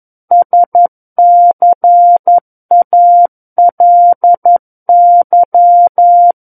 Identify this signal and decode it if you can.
Morse code, heard as a keyed tone, about 11 wpm, 700 Hz. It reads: SCALY
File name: 11 WPM